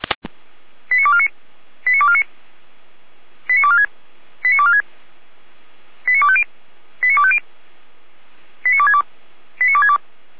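Five-tone selective call (ZVEI), heard as quick runs of five tones, tones of about 70 ms. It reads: 89249 89249 89246 89246 89250 89250 89262 89262
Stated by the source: ZVEI five-tone call 89230 received by SDR, from feuerwehr-orlamuende.org fire brigade server.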